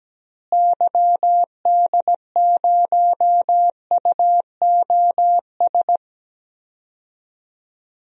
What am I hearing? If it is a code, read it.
Morse code, heard as a keyed tone, about 17 wpm, 700 Hz. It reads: YD0UOS
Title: DXCC-100 Call Signs - 6 Characters - Repeated 2x - 17wpm